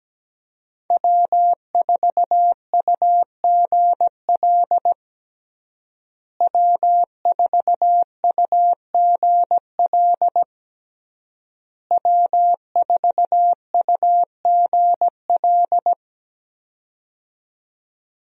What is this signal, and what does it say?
Morse code, heard as a keyed tone, about 17 wpm, 700 Hz. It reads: W4UGL W4UGL W4UGL
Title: USA Call Signs - 5 Characters - Repeated 3x - 17wpm